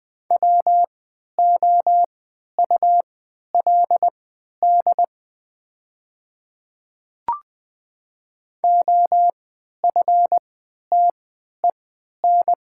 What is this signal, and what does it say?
Morse code, heard as a keyed tone, about 20 wpm, spaced as for 11 wpm, 700 Hz. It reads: WOULD OFTEN
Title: 3x Character Spacing - Top 300 Words - 20wpm